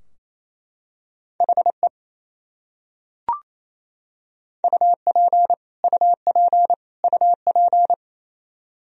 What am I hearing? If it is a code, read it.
Morse code, heard as a keyed tone, about 28 wpm, 700 Hz. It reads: HE UP UP UP
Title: Top 100 Words Repeated 28wpm